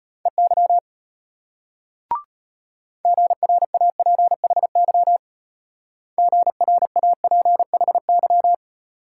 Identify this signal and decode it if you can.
Morse code, heard as a keyed tone, about 32 wpm, 700 Hz. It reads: EY GRAPHY GRAPHY